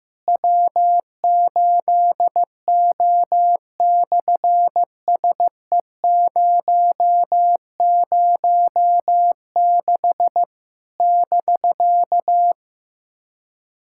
Morse code, heard as a keyed tone, about 15 wpm, 700 Hz. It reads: W8O/SE006 <BK>
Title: SOTA QSO Practice 15wpm